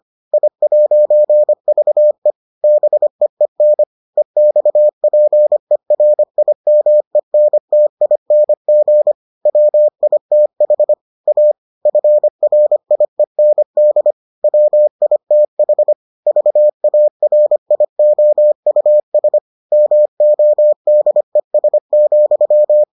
Morse code, heard as a keyed tone, about 25 wpm, 600 Hz. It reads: I'VE BEEN EXPERIMENTING WITH A FRIEND WITH VARIOUS MODES,